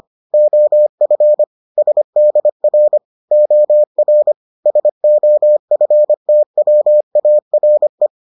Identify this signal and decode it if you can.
Morse code, heard as a keyed tone, about 25 wpm, 600 Hz. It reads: OF SDR OR SOFTWARE